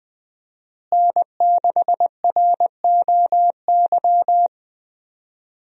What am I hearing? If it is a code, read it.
Morse code, heard as a keyed tone, about 20 wpm, 700 Hz. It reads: N6ROY
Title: USA Call Signs - 5 Characters - Repeated 3x - 20wpm